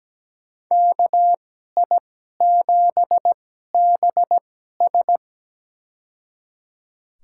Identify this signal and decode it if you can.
Morse code, heard as a keyed tone, about 17 wpm, spaced as for 12 wpm, 700 Hz. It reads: KI7BS
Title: USA Call Signs - 5 Characters - 2x Character Spacing - 17wpm